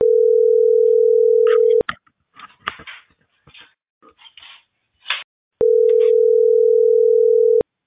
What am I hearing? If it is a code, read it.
Ringback tone (North America), 440 + 480 Hz.